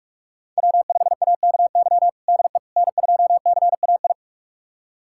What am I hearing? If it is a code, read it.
Morse code, heard as a keyed tone, about 46 wpm, 700 Hz. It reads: W5AKY DE N2CAI
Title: USA Callsign DE Callsign - 5 Characters - 45wpm